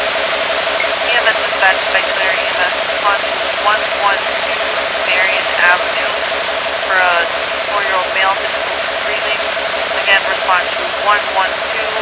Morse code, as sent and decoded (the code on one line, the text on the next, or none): none